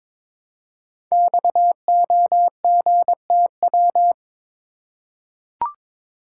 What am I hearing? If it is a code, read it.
Morse code, heard as a keyed tone, about 22 wpm, 700 Hz. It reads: XOGTW
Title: Code Groups - 5 Letter:Numbers - 22wpm